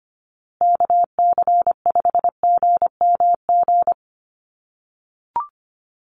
Morse code, heard as a keyed tone, about 25 wpm, 700 Hz. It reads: KC5GMG
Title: USA Call Signs Repeated 3x 25wpm